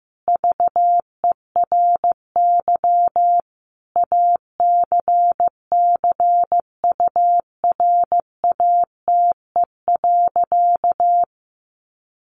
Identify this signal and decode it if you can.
Morse code, heard as a keyed tone, about 15 wpm, 700 Hz. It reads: VERY ACCURATE.
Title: Sentences from Top 1200 Words 15wpm